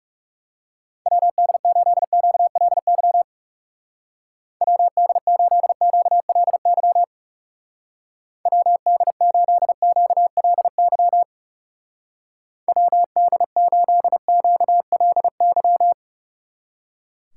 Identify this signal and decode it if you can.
Morse code, heard as a keyed tone, about 37 wpm, 700 Hz. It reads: WD8QLY WD8QLY WD8QLY WD8QLY
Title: Speed Racing - USA 2x3 Call Signs 30wpm